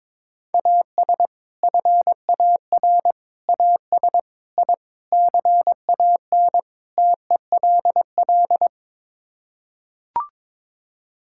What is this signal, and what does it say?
Morse code, heard as a keyed tone, about 22 wpm, 700 Hz. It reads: AS FAR AS I CAN TELL
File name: practice / 22wpm / Sets of 6 Words - 22wpm